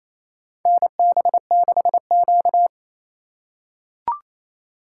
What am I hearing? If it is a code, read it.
Morse code, heard as a keyed tone, about 28 wpm, 700 Hz. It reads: NB6Q
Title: USA Call Signs - 2x Character Spacing - 28wpm